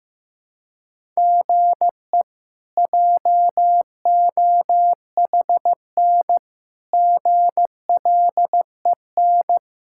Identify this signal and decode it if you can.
Morse code, heard as a keyed tone, about 15 wpm, 700 Hz. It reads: GE JOHN GLEN
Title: K1USN SST QSO Practice - 15wpm